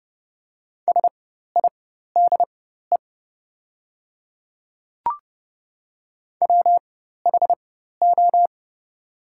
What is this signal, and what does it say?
Morse code, heard as a keyed tone, about 30 wpm, spaced as for 14 wpm, 700 Hz. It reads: SIDE WHO
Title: 4x Character Spacing - Top 100 Words - 30wpm